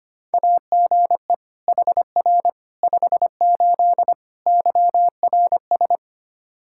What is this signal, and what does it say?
Morse code, heard as a keyed tone, about 25 wpm, 700 Hz. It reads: AGE HR 58 YRS
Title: Practice QSO - Exchange 3 - More Info - 25wpm